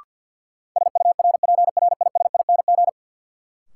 Morse code, heard as a keyed tone, about 50 wpm, 700 Hz. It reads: SURPRISING
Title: Sets of 5 Words - New Word Drill - 50wpm